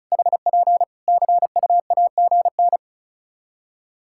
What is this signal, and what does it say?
Morse code, heard as a keyed tone, about 36 wpm, 700 Hz. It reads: HP CUAGN
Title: Practice QSO - Exchange 4 - Ending - 35wpm